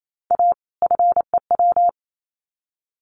Morse code, heard as a keyed tone, about 28 wpm, 700 Hz. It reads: A FEW
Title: Sets of 2 Words 28wpm